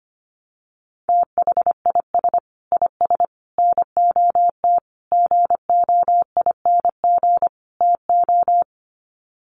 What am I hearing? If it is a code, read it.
Morse code, heard as a keyed tone, about 25 wpm, 700 Hz. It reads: THIS IS NOT GOING TO
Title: Sets of 5 Words Encore2 25wpm